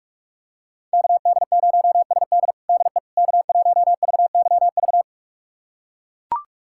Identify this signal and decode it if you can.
Morse code, heard as a keyed tone, about 42 wpm, 700 Hz. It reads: KD0SD DE K1VYV
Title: USA Callsign DE Callsign 45wpm